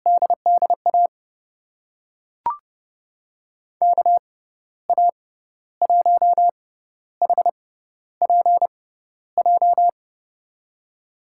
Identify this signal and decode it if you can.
Morse code, heard as a keyed tone, about 30 wpm, spaced as for 13 wpm, 700 Hz. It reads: DDA KA1HPJ
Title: USA Call Signs - 6 Characters - 6x Character Spacing - 30wpm